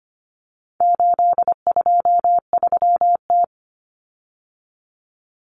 Morse code, heard as a keyed tone, about 25 wpm, 700 Hz. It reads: 823T
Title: Common Cut Numbers - 4-Digit 25wpm